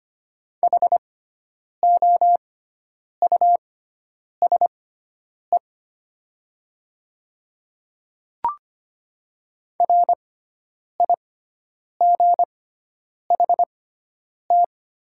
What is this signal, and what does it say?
Morse code, heard as a keyed tone, about 25 wpm, spaced as for 9 wpm, 700 Hz. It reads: HOUSE RIGHT
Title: 6x Character Spacing - Top 300 Words - 25wpm